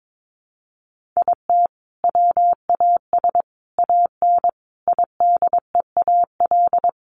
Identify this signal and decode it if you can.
Morse code, heard as a keyed tone, about 22 wpm, 700 Hz. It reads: IT WAS AN IDEAL
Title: Sentences from Top 1400 Words 22wpm